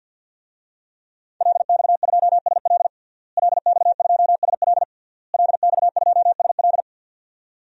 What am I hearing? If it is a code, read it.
Morse code, heard as a keyed tone, about 46 wpm, 700 Hz. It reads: LX2SL LX2SL LX2SL
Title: DXCC-100 Call Signs - 5 Characters - Repeated 3x - 50wpm